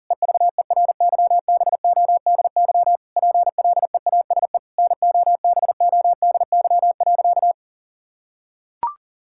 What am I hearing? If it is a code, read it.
Morse code, heard as a keyed tone, about 37 wpm, 700 Hz. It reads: EVERYBODY PLEASE NOBODY.
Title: Sentences from Top 1600 Words 40wpm